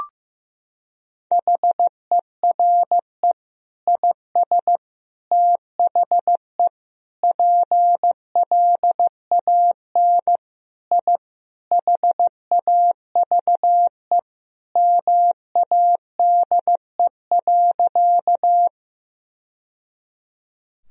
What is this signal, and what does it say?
Morse code, heard as a keyed tone, about 15 wpm, 700 Hz. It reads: HERE IS THE PLAN I HAVE MADE.
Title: Sentences from Top 1000 Words - Review Part 2 - 15wpm